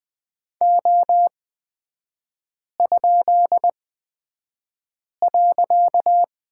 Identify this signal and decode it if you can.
Morse code, heard as a keyed tone, about 20 wpm, spaced as for 5 wpm, 700 Hz. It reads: O?.
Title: 255 Lesson 41e - Period - Three Characters 20wpm